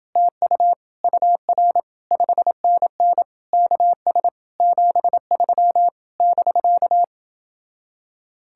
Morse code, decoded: TU UR 5NN KS 73 <BK>

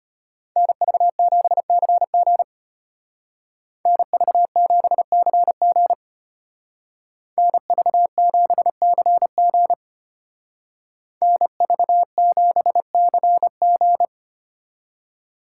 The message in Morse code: NV7CG NV7CG NV7CG NV7CG

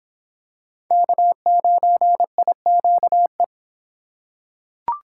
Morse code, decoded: K9IQE

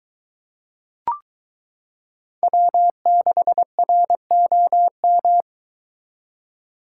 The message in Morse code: W6ROM

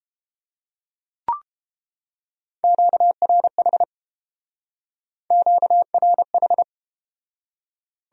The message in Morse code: QRH QRH

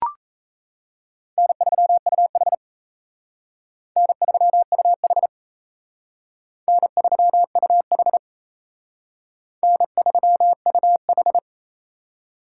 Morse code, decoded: N3UH N3UH N3UH N3UH

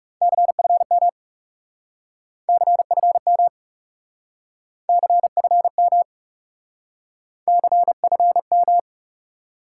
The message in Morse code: CFM CFM CFM CFM

